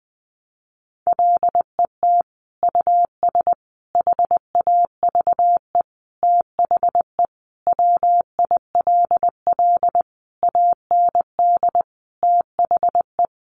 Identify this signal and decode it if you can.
Morse code, heard as a keyed tone, about 20 wpm, 700 Hz. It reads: LET US HAVE THE WILL AND THE
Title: Sentences from Top 1300 Words 20wpm